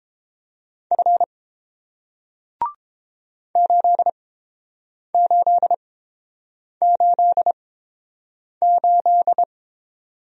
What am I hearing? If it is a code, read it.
Morse code, heard as a keyed tone, about 30 wpm, spaced as for 7 wpm, 690 Hz. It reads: F 8888